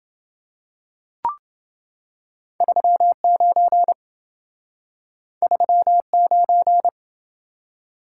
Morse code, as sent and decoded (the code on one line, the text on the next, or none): ...-- ----. / ...-- ----.
39 39